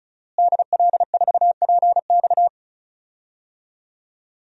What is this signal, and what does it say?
Morse code, heard as a keyed tone, about 36 wpm, 700 Hz. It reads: DL4PX
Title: DXCC-100 Call Signs - 5 Characters - Repeated 3x - 35wpm